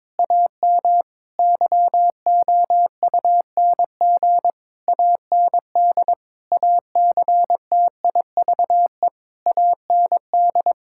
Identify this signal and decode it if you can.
Morse code, heard as a keyed tone, about 22 wpm, 700 Hz. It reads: AM YOUNG AND ACTIVE AND
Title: Sentences from Top 1000 Words - Review Part 6 - 22wpm